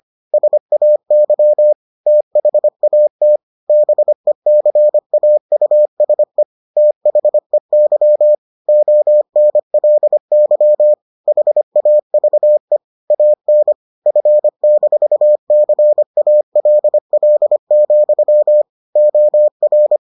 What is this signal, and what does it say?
Morse code, heard as a keyed tone, about 25 wpm, 600 Hz. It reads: SAY THAT BECAUSE THEY ONLY HAVE AN F-CALL, OR